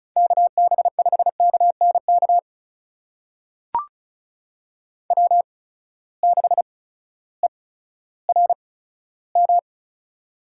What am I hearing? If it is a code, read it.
Morse code, heard as a keyed tone, about 36 wpm, spaced as for 16 wpm, 700 Hz. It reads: KB5KNK W6ERM
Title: USA Call Signs - 8x Character Spacing - 35wpm